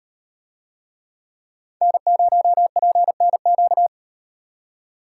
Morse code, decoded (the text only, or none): N0PNQ